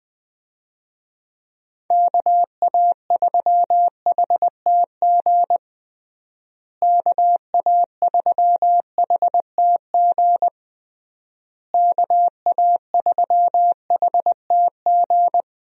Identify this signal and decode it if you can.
Morse code, heard as a keyed tone, about 20 wpm, 700 Hz. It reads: KA3HTG KA3HTG KA3HTG
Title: USA Call Signs - 6 Characters - Repeated 3x - 20wpm